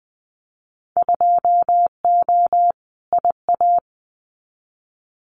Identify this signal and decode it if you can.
Morse code, heard as a keyed tone, about 20 wpm, 700 Hz. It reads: 2O IA
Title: Winter Field Day Exchanges 20wpm